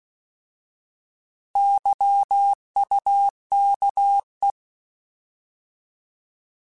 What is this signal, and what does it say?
Morse code, heard as a keyed tone, about 16 wpm, 800 Hz. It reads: YUKE
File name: Alphabetical / Alphabetical mp3s / Ys @15